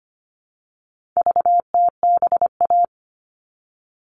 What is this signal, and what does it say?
Morse code, heard as a keyed tone, about 25 wpm, 700 Hz. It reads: VTBA
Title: All Cut Numbers - 4-Digit 25wpm